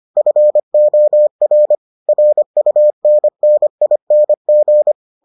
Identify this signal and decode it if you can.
Morse code, heard as a keyed tone, about 25 wpm, 600 Hz. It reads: FOR RUNNING